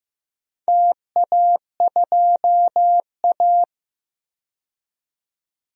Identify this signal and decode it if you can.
Morse code, heard as a keyed tone, about 15 wpm, 700 Hz. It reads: TA2A
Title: DXCC-100 Call Signs - 4 Characters - 15wpm